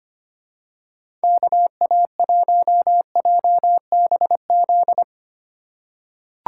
Morse code, heard as a keyed tone, about 25 wpm, 700 Hz. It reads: KA1JBZ
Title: USA Call Signs - 6 Characters - 4x Character Spacing - 25wpm